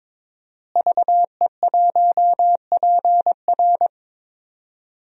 Morse code, heard as a keyed tone, about 21 wpm, 700 Hz. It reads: VE1PR
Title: DXCC-100 Call Signs - 5 Characters - Repeated 3x - 22wpm